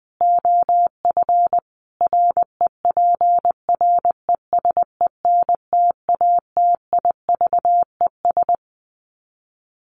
Morse code, heard as a keyed tone, about 20 wpm, 700 Hz. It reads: OF REPRESENTATIVES
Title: Sets of 5 Words Encore 20wpm